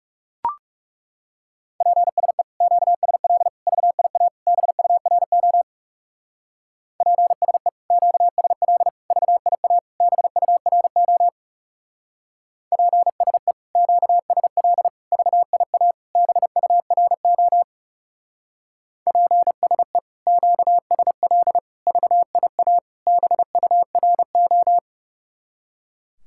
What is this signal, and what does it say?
Morse code, heard as a keyed tone, about 38 wpm, 700 Hz. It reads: PSE QSL VIA BURO PSE QSL VIA BURO PSE QSL VIA BURO PSE QSL VIA BURO